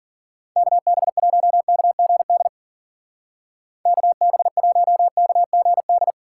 Morse code, heard as a keyed tone, about 50 wpm, 700 Hz. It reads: KB1KGD KB1KGD
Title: Speed Racing - USA 2x3 Call Signs 40wpm